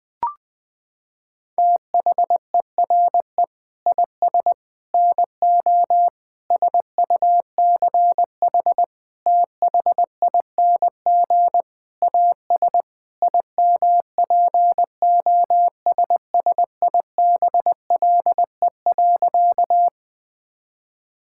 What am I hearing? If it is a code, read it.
Morse code, heard as a keyed tone, about 21 wpm, 700 Hz. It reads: THERE IS NO SUCH THING AS IMPOSSIBLE.
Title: Sentences with 6-Word Phrases - 20wpm